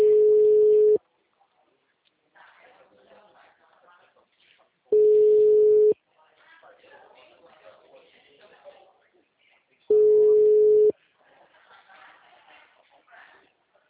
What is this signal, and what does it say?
Ringback tone (Europe, 425 Hz), with speech.